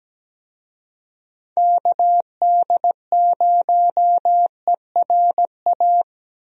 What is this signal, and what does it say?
Morse code, heard as a keyed tone, about 17 wpm, 700 Hz. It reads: KD0ERA
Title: USA Call Signs - 6 Characters - Repeated 2x - 17wpm